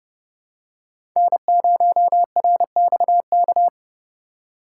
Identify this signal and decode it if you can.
Morse code, heard as a keyed tone, about 30 wpm, 700 Hz. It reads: N0RXK
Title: USA Call Signs - 5 Characters - 4x Character Spacing - 30wpm